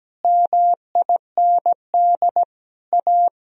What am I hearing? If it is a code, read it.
Morse code, heard as a keyed tone, about 17 wpm, 700 Hz. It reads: MIND A
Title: Sentences from Top 1000 Words - Review Part 1 - 17wpm